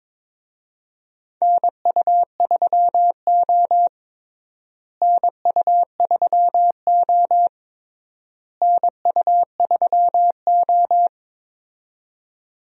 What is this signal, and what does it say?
Morse code, heard as a keyed tone, about 22 wpm, 700 Hz. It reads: NU3O NU3O NU3O
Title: USA Call Signs - 4 Characters - Repeated 3x - 22wpm